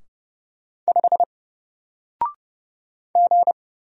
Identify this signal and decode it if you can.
Morse code, heard as a keyed tone, about 30 wpm, 700 Hz. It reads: 5 G
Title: Single Letter-Number Alt 30wpm